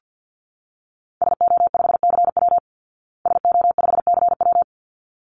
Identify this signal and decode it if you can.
Morse code, heard as a keyed tone, about 48 wpm, 700 Hz. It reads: SO5CW SO5CW
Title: Call Signs Repeated 3x - 50wpm